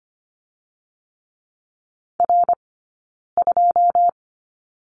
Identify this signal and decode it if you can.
Morse code, heard as a keyed tone, about 25 wpm, 700 Hz. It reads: R 2
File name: course / 25:10wpm / 092 Lesson 16d - Letter c - Two Characters 25wpm